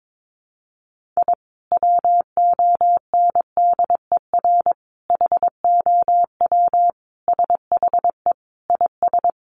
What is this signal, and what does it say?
Morse code, heard as a keyed tone, about 22 wpm, 700 Hz. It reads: I WONDER HOW SHE IS
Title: Sentences from Top 1500 Words 22wpm